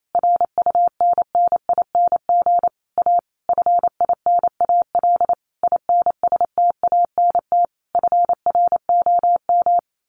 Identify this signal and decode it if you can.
Morse code, heard as a keyed tone, about 28 wpm, 700 Hz. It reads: RUNNING A FINAL INSTANT FROM
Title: A Sound Of Thunder 28wpm